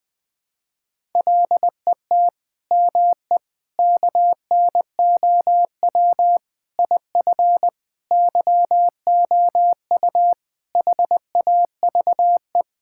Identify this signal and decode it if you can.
Morse code, heard as a keyed tone, about 20 wpm, 700 Hz. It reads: LET ME KNOW IF YOU HAVE
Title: Sentences with 8-Word Phrases - Spoken Beginning - 20wpm